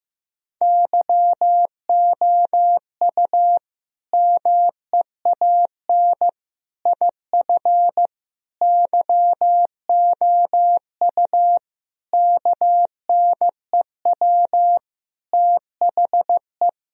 Morse code, heard as a keyed tone, about 15 wpm, 700 Hz. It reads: YOU MEAN IF YOU KNEW THE
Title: Sentences from Top 1000 Words - Review Part 4 - 15wpm